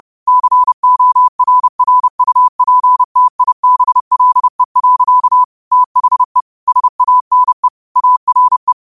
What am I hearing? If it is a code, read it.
Morse code, heard as a keyed tone, about 30 wpm, 1000 Hz. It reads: CORRUPTIBLE. THE SANE ARE
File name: qotdfiles / 30wpm